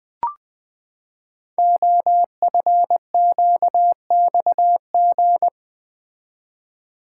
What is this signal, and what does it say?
Morse code, heard as a keyed tone, about 20 wpm, 700 Hz. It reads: OFQXG
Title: Code Groups - 5 Letters - 20wpm